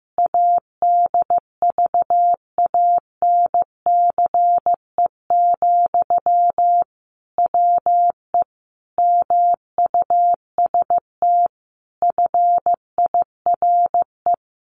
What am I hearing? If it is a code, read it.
Morse code, heard as a keyed tone, about 14 wpm, 700 Hz. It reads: ADVANCE, WE MUST FIRE
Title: Sentences from Top 1400 Words 15wpm